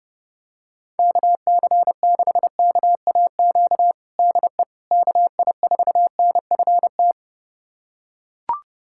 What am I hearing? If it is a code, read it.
Morse code, heard as a keyed tone, about 30 wpm, 700 Hz. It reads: KC6KAQ DE KI4NFT
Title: USA Callsign DE Callsign - 6 Characters - 30wpm